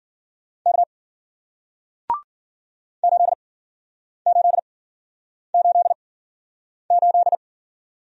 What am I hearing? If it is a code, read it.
Morse code, heard as a keyed tone, about 55 wpm, spaced as for 9 wpm, 700 Hz. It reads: K 8888